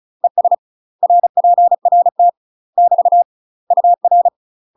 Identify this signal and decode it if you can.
Morse code, heard as a keyed tone, about 34 wpm, 700 Hz. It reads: ES RPRT <BT> UR
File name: UK MASTER QSOs TRAINING / 35WPM / UK MASTER QSOs 0042